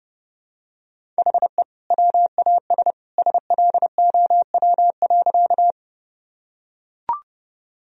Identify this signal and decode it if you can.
Morse code, heard as a keyed tone, about 30 wpm, 700 Hz. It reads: HE WAS SLOW.
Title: Sentences from Top 1000 Words - Review Part 2 - 30wpm